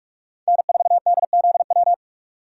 Morse code, heard as a keyed tone, about 42 wpm, 700 Hz. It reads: N4DZW
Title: USA Call Signs - 3x Character Spacing - 45wpm